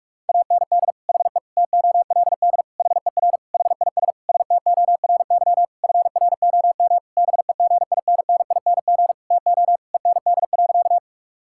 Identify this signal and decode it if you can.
Morse code, heard as a keyed tone, about 46 wpm, 690 Hz. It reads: AND HE TOLD HER HIS STORY FROM BEGINNING TO END.
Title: Sentences from Top 1000 Words - Review Part 5 - 45wpm